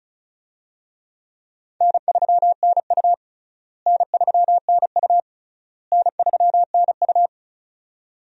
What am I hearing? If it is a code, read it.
Morse code, heard as a keyed tone, about 36 wpm, 700 Hz. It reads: N3NU N3NU N3NU
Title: USA Call Signs - 4 Characters - Repeated 3x - 35wpm